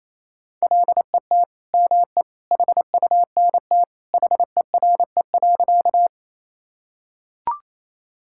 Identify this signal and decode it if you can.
Morse code, heard as a keyed tone, about 28 wpm, 700 Hz. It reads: LET ME HUNT HERE.
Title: Sentences from Top 1600 Words 28wpm